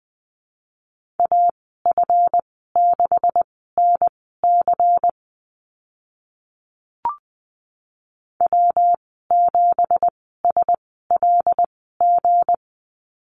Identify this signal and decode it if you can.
Morse code, heard as a keyed tone, about 20 wpm, spaced as for 14 wpm, 700 Hz. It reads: AF6NC W7SLG
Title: 186 Lesson 30g - Letter b - Call Signs 20wpm